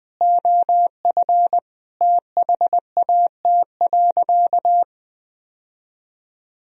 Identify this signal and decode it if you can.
Morse code, heard as a keyed tone, about 20 wpm, 700 Hz. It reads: OF THAT.
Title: Sentences from Top 1000 Words - Review Part 2 - 20wpm